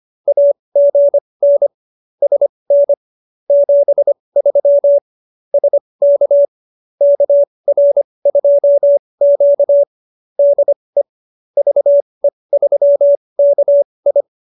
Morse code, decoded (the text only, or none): AGN SN 73 SK KR2Q DE VE3KI